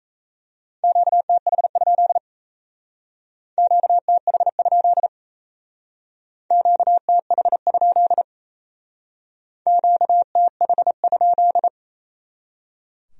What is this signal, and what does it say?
Morse code, heard as a keyed tone, about 37 wpm, 700 Hz. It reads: QTH? QTH? QTH? QTH?